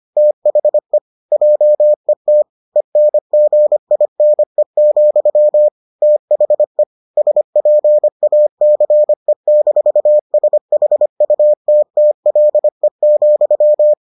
Morse code, heard as a keyed tone, about 25 wpm, 600 Hz. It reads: THE JET ENGINE, THE SPACE-SHUTTLE,